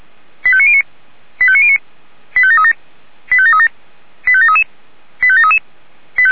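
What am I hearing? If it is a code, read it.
Five-tone selective call (ZVEI), heard as quick runs of five tones, tones of about 70 ms. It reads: 85909 85909 85737 85737 85733 85733
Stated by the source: SDR receiver recording of ZVEI five-tone selective call, code 85758, for Feuerwehr Orlamünde fire brigade.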